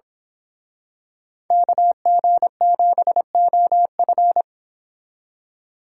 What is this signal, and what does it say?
Morse code, heard as a keyed tone, about 27 wpm, 700 Hz. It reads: KG7OF
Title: Speed Racing - USA 2x2 Call Signs 22wpm